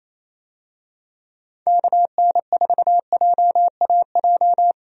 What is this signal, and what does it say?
Morse code, heard as a keyed tone, about 28 wpm, 700 Hz. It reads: KN4JAJ
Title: USA Call Signs - 6 Characters - 28wpm